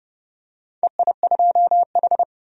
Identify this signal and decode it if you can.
Morse code, heard as a keyed tone, about 30 wpm, 700 Hz. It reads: EI2H
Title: DXCC-100 Call Signs - 4 Characters - 30wpm